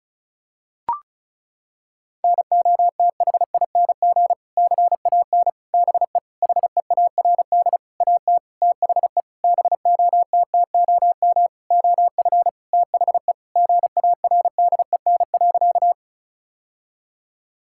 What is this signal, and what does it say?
Morse code, heard as a keyed tone, about 36 wpm, 700 Hz. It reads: NOTHING CAN BE HEARD AT THE BOTTOM OF THE GARDEN.
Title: Sentences from Top 1000 Words - Review Part 6 - 35wpm